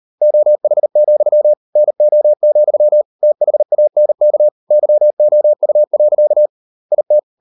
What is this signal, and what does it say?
Morse code, heard as a keyed tone, about 38 wpm, 600 Hz. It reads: OH, NO, THANK YOU. IT